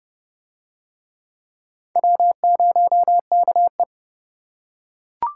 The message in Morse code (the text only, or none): W0KE